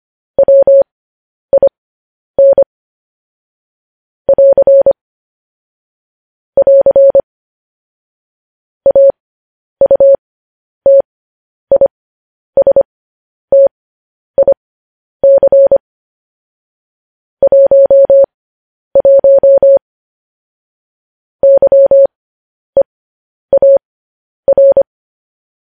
Morse code, heard as a keyed tone, about 25 wpm, spaced as for 10 wpm, 550 Hz. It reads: WIN <AR> <AR> AUTISTIC 11 YEAR
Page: news headlines in morse code at 10 wpm